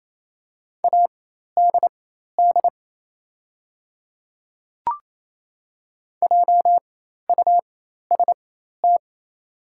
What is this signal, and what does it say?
Morse code, heard as a keyed tone, about 27 wpm, spaced as for 13 wpm, 700 Hz. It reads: ADD JUST
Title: 4x Character Spacing - Top 200 Words - 28wpm